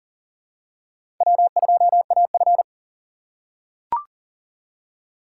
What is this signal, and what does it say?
Morse code, heard as a keyed tone, about 37 wpm, 700 Hz. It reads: W2AF
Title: USA Call Signs - 4 Characters - 4x Character Spacing - 40wpm